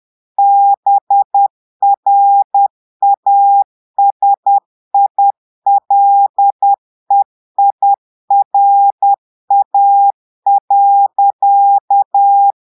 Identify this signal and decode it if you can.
Morse code, heard as a keyed tone, about 10 wpm, 800 Hz. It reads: BRASILEIRA.
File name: Portugais - texte en clair 1 - vitesse - 10 mots minute